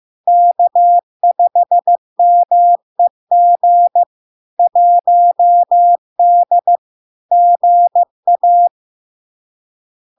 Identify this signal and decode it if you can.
Morse code, heard as a keyed tone, about 15 wpm, 700 Hz. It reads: K5MEG 1D GA
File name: LARK-FD-2022-Log-in-CW-15wpm-700-Hz